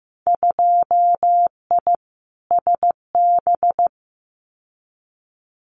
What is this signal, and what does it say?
Morse code, heard as a keyed tone, about 15 wpm, 700 Hz. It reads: 2I SB
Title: Winter Field Day Exchanges Encore 15wpm